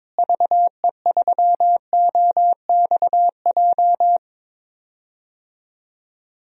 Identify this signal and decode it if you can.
Morse code, heard as a keyed tone, about 21 wpm, 700 Hz. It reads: VE3OXJ
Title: DXCC-100 Call Signs - 6 Characters - Repeated 3x - 22wpm